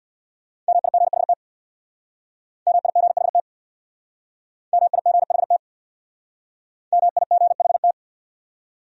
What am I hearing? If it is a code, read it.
Morse code, heard as a keyed tone, about 62 wpm, 700 Hz. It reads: MIGHT MIGHT MIGHT MIGHT
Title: Speed-Racing Top 500 Words 50wpm